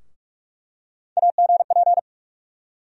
Morse code, heard as a keyed tone, about 44 wpm, 700 Hz. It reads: AGP
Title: USA Call Sign Postfixes - 3 Letters 45wpm